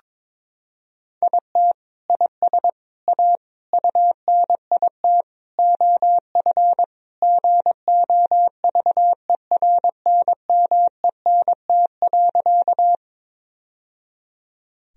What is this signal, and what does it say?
Morse code, heard as a keyed tone, about 22 wpm, 700 Hz. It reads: IT IS A UNIT OF GOVERNMENT.
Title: Sentences from Top 1000 Words - Review Part 2 - 22wpm